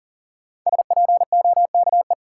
Morse code, heard as a keyed tone, about 40 wpm, 700 Hz. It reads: SPOKE
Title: Sentences from Top 1200 Words - New Word Drill - 40wpm